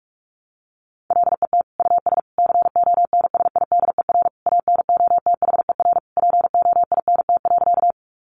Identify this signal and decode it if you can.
Morse code, heard as a keyed tone, about 46 wpm, 700 Hz. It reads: LET US CONSIDER ANOTHER POINT.
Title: Sentences from Top 600 Words 45wpm